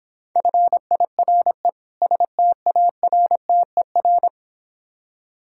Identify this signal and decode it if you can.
Morse code, heard as a keyed tone, about 24 wpm, 700 Hz. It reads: FIRE STARTER
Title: Difficult Words - Speed-Racing with Hints - 20wpm